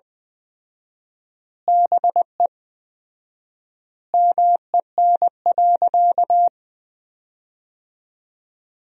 Morse code, heard as a keyed tone, about 20 wpm, 700 Hz. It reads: BE MEN.
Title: 4x Word-Spacing - Sentences from Top 300 Words - 20wpm